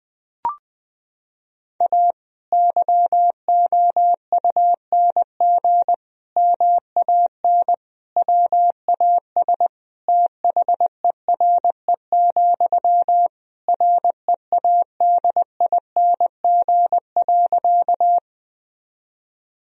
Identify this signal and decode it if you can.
Morse code, heard as a keyed tone, about 20 wpm, 700 Hz. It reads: A YOUNG MAN WAS THERE, READING.
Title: Sentences from Top 1000 Words - Review Part 3 - 20wpm